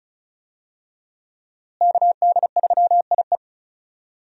Morse code, heard as a keyed tone, about 36 wpm, 700 Hz. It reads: KD3IE
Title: USA Call Signs - 5 Characters - Repeated 3x - 35wpm